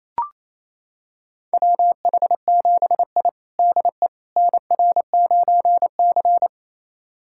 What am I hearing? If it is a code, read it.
Morse code, heard as a keyed tone, about 28 wpm, 700 Hz. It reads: WH7I DE NR9C
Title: USA Callsign DE Callsign - 4 Characters - 28wpm